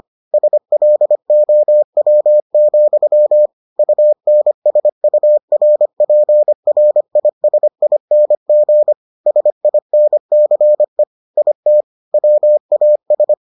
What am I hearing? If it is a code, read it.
Morse code, heard as a keyed tone, about 25 wpm, 600 Hz. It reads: SLOW, UNSURPRISING SINCE IT WAS